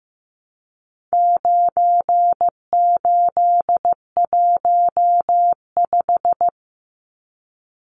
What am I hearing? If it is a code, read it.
Morse code, heard as a keyed tone, about 15 wpm, 700 Hz. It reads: 9815
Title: Numbers - 4-Digit 15wpm